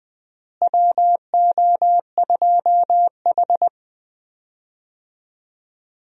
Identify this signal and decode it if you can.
Morse code, heard as a keyed tone, about 20 wpm, 700 Hz. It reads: WO2H
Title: USA Call Signs - 4 Characters - Repeated 3x - 20wpm